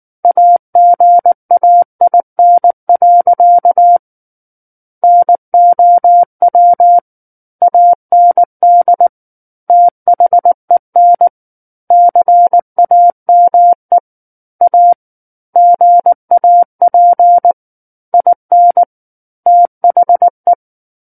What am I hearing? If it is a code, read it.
Morse code, heard as a keyed tone, about 19 wpm, 700 Hz. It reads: AGAIN. NOW AND THEN CAME A GAP IN THE
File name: Book One / 10-Chapter 10 - 19 WPM